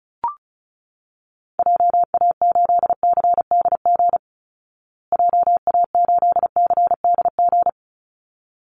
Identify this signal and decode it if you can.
Morse code, heard as a keyed tone, about 34 wpm, 700 Hz. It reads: JA8CDG JA8CDG
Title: Call Signs Repeated 3x - 35wpm